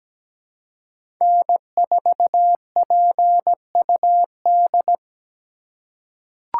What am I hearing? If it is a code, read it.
Morse code, heard as a keyed tone, about 17 wpm, 700 Hz. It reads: N4PUD
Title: USA Call Signs - 2x Character Spacing - 17wpm